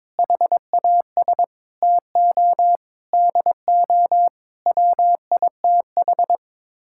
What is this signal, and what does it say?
Morse code, heard as a keyed tone, about 21 wpm, 700 Hz. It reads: HAS TO DO WITH
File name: practice / 22wpm / Sets of 8 Words - 22wpm